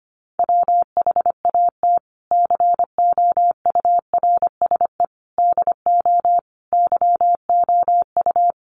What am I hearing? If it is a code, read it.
Morse code, heard as a keyed tone, about 25 wpm, 700 Hz. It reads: WHAT COURSE DO YOU
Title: Sentences from Top 1200 Words 25wpm